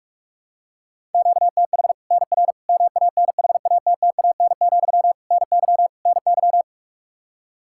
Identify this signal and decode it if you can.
Morse code, heard as a keyed tone, about 46 wpm, 700 Hz. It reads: QTH NR MANHATTAN, NY NY
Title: Practice QSO - Exchange 1 - Beginning - 45wpm